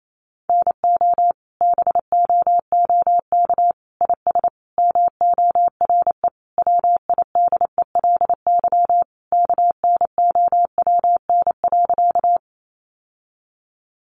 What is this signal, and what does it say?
Morse code, heard as a keyed tone, about 28 wpm, 700 Hz. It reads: NO BOOK IS MORE WIDELY KNOWN.
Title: Sentences from Top 1200 Words 28wpm